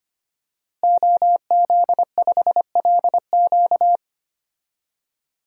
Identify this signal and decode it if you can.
Morse code, heard as a keyed tone, about 25 wpm, 700 Hz. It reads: OZ5LQ
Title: DXCC-100 Call Signs - 5 Characters - Repeated 3x - 25wpm